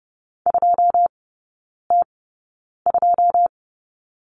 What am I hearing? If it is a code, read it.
Morse code, heard as a keyed tone, about 30 wpm, spaced as for 9 wpm, 700 Hz. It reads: 2T2
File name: 081 Lesson 14e - Number 2 - Three Characters 30wpm